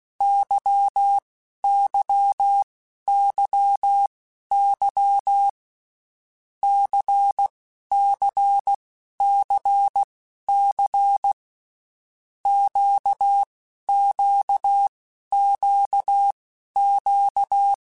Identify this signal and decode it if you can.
Morse code, heard as a keyed tone, about 16 wpm, spaced as for 11 wpm, 800 Hz. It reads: YYYY CCCC QQQQ
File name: Groups@10 / Groups@10 mp3s / LFCYQ@10